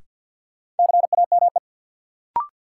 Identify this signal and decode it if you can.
Morse code, heard as a keyed tone, about 47 wpm, 700 Hz. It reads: CAME
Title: Sentences from Top 200 Words - New Word Drill - 50wpm